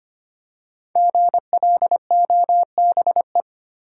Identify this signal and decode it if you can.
Morse code, heard as a keyed tone, about 25 wpm, 700 Hz. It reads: GLOBE
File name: 5 Letter Words - First 3 Letters Given - 25wpm